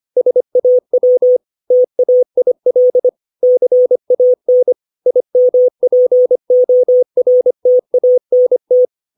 Morse code, heard as a keyed tone, about 25 wpm, 500 Hz. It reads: SAW TAIL CAN IMPORTANT
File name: 632 words / 632 Words 25WPM